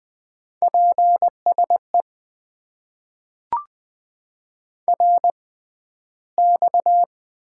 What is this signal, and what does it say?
Morse code, heard as a keyed tone, about 20 wpm, spaced as for 12 wpm, 700 Hz. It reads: PSE RX
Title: Top 90 QSO Elements - 6x Character Spacing - 20wpm